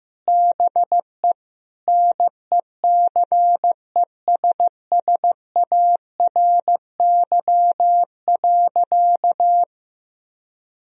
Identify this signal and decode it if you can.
Morse code, heard as a keyed tone, about 16 wpm, 700 Hz. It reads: BE NECESSARY.
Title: Sentences from Top 1000 Words - Review Part 3 - 15wpm